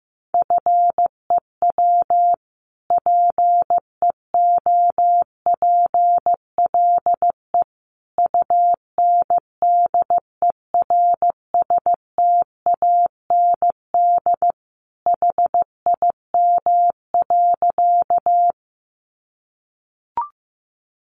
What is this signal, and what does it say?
Morse code, heard as a keyed tone, about 15 wpm, 700 Hz. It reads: FEW PEOPLE UNDERSTAND HIM.
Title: Sentences from Top 600 Words 15wpm